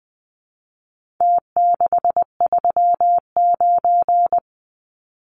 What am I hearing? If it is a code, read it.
Morse code, heard as a keyed tone, about 20 wpm, 700 Hz. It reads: T639
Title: Numbers - Contest Style - 4-Digit 20wpm